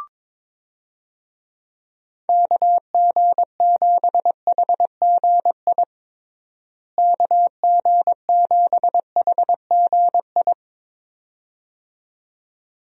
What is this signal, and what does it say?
Morse code, heard as a keyed tone, about 21 wpm, 700 Hz. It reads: KG7HGI KG7HGI
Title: USA Call Signs - 6 Characters - Repeated 2x - 22wpm